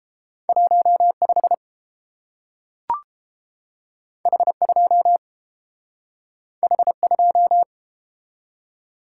Morse code, heard as a keyed tone, about 30 wpm, 700 Hz. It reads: 15 H2 H2